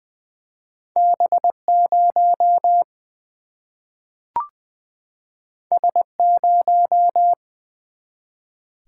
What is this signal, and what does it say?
Morse code, heard as a keyed tone, about 21 wpm, 700 Hz. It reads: B 0 S 0
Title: Single-Letter Single-Number 20wpm